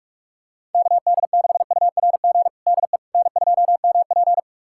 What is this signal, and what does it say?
Morse code, heard as a keyed tone, about 43 wpm, 700 Hz. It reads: KD6URG DE N2MP
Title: USA Callsign DE Callsign 45wpm